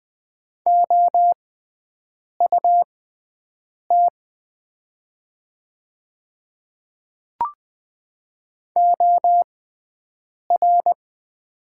Morse code, heard as a keyed tone, about 20 wpm, spaced as for 7 wpm, 700 Hz. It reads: OUT OR
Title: 6x Character Spacing - Top 100 Words - 20wpm